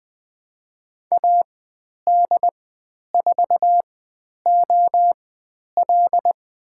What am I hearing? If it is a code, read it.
Morse code, heard as a keyed tone, about 20 wpm, spaced as for 10 wpm, 700 Hz. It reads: AD4OL
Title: 102 Lesson 17g - Letter u - Call Signs 20wpm